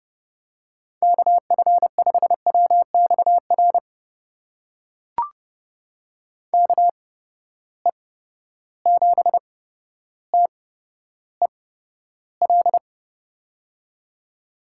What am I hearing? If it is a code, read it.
Morse code, heard as a keyed tone, about 30 wpm, spaced as for 13 wpm, 700 Hz. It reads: KF5WXR KE7TEL
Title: USA Call Signs - 6 Characters - 8x Character Spacing - 30wpm